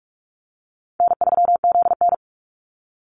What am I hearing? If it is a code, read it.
Morse code, heard as a keyed tone, about 44 wpm, 700 Hz. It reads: N3ZN
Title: Easy Call Signs Repeated 3x - 45wpm